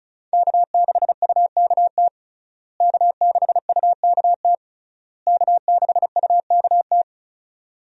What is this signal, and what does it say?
Morse code, heard as a keyed tone, about 36 wpm, 700 Hz. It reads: K6UKT K6UKT K6UKT
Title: USA Call Signs - 5 Characters - Repeated 3x - 35wpm